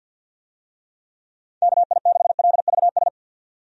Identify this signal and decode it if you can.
Morse code, heard as a keyed tone, about 51 wpm, 700 Hz. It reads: KI6LVS
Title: USA Call Signs - 6 Characters - 50wpm